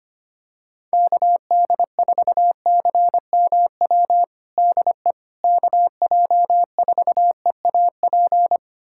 Morse code, heard as a keyed tone, about 25 wpm, 700 Hz. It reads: KD4CMW DE KJ4EAP
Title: USA Callsign DE Callsign 25wpm